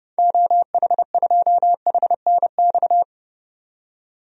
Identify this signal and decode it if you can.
Morse code, heard as a keyed tone, about 31 wpm, 690 Hz. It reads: OH2HNX
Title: DXCC-100 Call Signs - 6 Characters - 30wpm